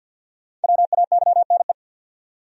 Morse code, heard as a keyed tone, about 47 wpm, 700 Hz. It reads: WAYNE
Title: USA Common Names - Max 5 Letters - 50wpm